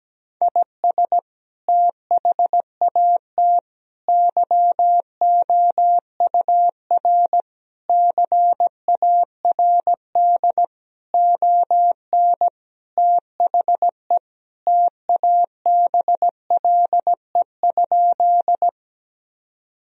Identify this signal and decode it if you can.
Morse code, heard as a keyed tone, about 17 wpm, 700 Hz. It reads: IS THAT YOUR CARD ON THE TABLE?
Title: Sentences from Top 1000 Words - Review Part 4 - 17wpm